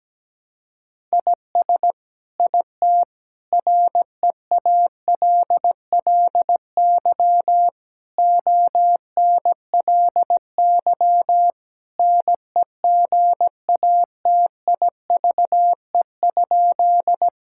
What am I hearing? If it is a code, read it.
Morse code, heard as a keyed tone, about 17 wpm, 700 Hz. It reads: IS IT REALLY ONLY NEGATIVE?
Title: Sentences from Top 1000 Words - Review Part 6 - 17wpm